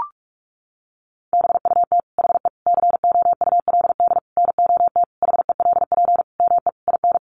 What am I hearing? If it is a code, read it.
Morse code, heard as a keyed tone, about 44 wpm, 700 Hz. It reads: BUT HE COULD NOT HELP ME IN
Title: Sentences from Top 600 Words 45wpm